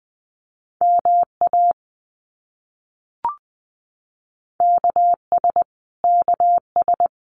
Morse code, heard as a keyed tone, about 20 wpm, 700 Hz. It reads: MA KS KS
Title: US States Repeated 3x 20wpm